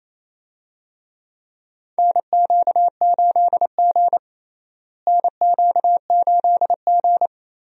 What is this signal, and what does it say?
Morse code, heard as a keyed tone, about 28 wpm, 700 Hz. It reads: NQ8G NQ8G
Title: USA Call Signs - 4 Characters - Repeated 2x - 28wpm